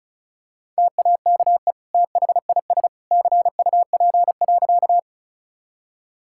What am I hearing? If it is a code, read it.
Morse code, heard as a keyed tone, about 36 wpm, 700 Hz. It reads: TAKE THIS CUP.
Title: Sentences from Top 1600 Words 35wpm